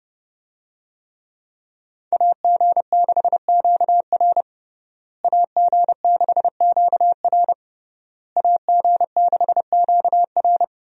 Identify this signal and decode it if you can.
Morse code, heard as a keyed tone, about 30 wpm, 700 Hz. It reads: AG6QR AG6QR AG6QR
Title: USA Call Signs - 5 Characters - Repeated 3x - 30wpm